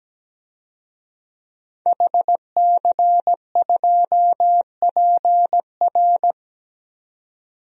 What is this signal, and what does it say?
Morse code, heard as a keyed tone, about 17 wpm, 700 Hz. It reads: HC2PR
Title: DXCC-100 Call Signs - 5 Characters - 17wpm